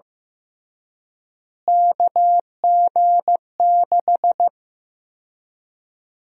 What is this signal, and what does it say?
Morse code, heard as a keyed tone, about 15 wpm, 700 Hz. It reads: KG6
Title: USA Call Sign Prefixes - Repeated 2x - 15wpm